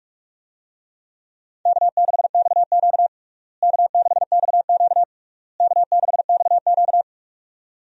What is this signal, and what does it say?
Morse code, heard as a keyed tone, about 46 wpm, 700 Hz. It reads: K6XQ K6XQ K6XQ
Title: USA Call Signs - 4 Characters - Repeated 3x - 45wpm